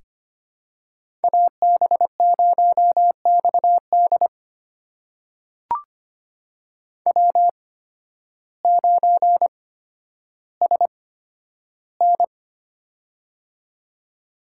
Morse code, heard as a keyed tone, about 25 wpm, spaced as for 12 wpm, 700 Hz. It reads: AB0XD W9SN
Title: USA Call Signs - 8x Character Spacing - 25wpm